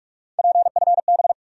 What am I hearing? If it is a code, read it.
Morse code, heard as a keyed tone, about 46 wpm, 700 Hz. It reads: PFB
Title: USA Call Sign Postfixes - 3 Letters 45wpm